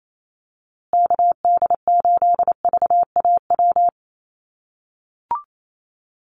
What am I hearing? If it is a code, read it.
Morse code, heard as a keyed tone, about 28 wpm, 700 Hz. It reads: KD8VAW
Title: USA Call Signs Repeated 3x 28wpm